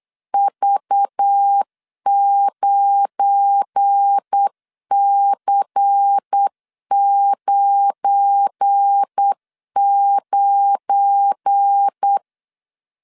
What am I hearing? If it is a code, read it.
Morse code, heard as a keyed tone, about 8 wpm, 800 Hz. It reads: V9C99
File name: LEZIONI / 7 GF9Z / 7 GF9 8wpm 3545